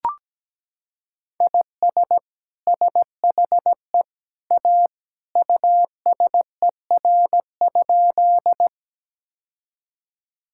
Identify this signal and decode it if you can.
Morse code, heard as a keyed tone, about 17 wpm, 700 Hz. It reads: IS SHE A USER?
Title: Sentences from Top 1000 Words - Review Part 5 - 17wpm